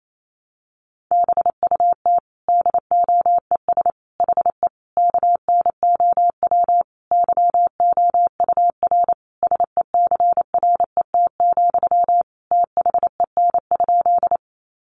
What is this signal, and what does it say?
Morse code, heard as a keyed tone, about 28 wpm, 700 Hz. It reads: BUT DOES HE KNOW YOUR SECRET, THEN?
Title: Sentences from Top 1200 Words 28wpm